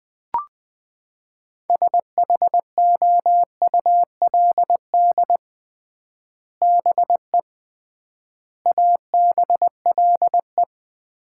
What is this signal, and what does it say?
Morse code, heard as a keyed tone, about 21 wpm, 700 Hz. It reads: SHOULD BE ABLE
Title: 3x Word-Spacing - Sets of 4 Words - 20wpm